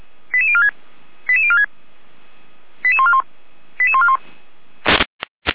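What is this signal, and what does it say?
Five-tone selective call (ZVEI), heard as quick runs of five tones, tones of about 70 ms. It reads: 88046 88046 88141 88141
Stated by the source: ZVEI five-tone selective call received on SDR by feuerwehr-orlamuende.org (fire brigade Orlamünde).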